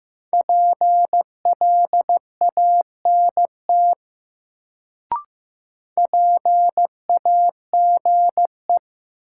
Morse code, heard as a keyed tone, about 15 wpm, 700 Hz. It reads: PLANT PAGE
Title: Sentences from Top 1000 Words - Review Part 1 - New Word Drill - 15wpm